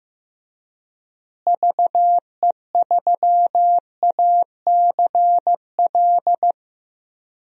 Morse code, heard as a keyed tone, about 15 wpm, 700 Hz. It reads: VE3ACL
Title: DXCC-100 Call Signs - 6 Characters - Repeated 3x - 15wpm